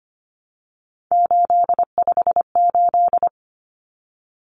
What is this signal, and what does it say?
Morse code, heard as a keyed tone, about 25 wpm, 700 Hz. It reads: 858
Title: Numbers - 3-Digit 25wpm